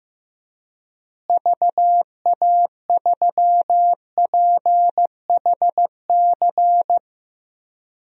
Morse code, heard as a keyed tone, about 15 wpm, 700 Hz. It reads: VA3PHC
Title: DXCC-100 Call Signs - 6 Characters - Repeated 3x - 15wpm